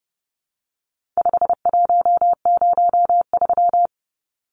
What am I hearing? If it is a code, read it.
Morse code, heard as a keyed tone, about 30 wpm, 700 Hz. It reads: 5103
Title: Numbers - 4-Digit 30wpm